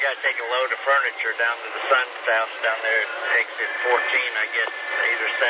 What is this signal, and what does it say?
Emergency Alert System (SAME) tones, heard as harsh buzzing bursts, with speech. It is heard from 1.1 to 4.5 s.